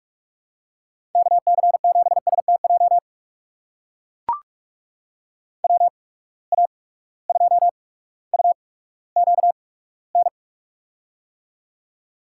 Morse code, heard as a keyed tone, about 44 wpm, spaced as for 19 wpm, 700 Hz. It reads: KC7STJ WA2UQN
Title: USA Call Signs - 6 Characters - 8x Character Spacing - 45wpm